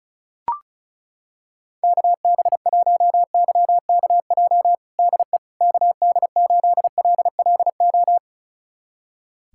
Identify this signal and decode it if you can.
Morse code, heard as a keyed tone, about 34 wpm, 700 Hz. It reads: KB1YKJ DE KD8LLO
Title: USA Callsign DE Callsign - 6 Characters - 35wpm